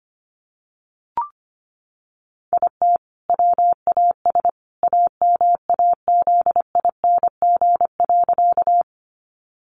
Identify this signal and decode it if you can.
Morse code, heard as a keyed tone, about 25 wpm, 700 Hz. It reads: IT WAS AMAZING.
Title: Sentences from Top 1500 Words 25wpm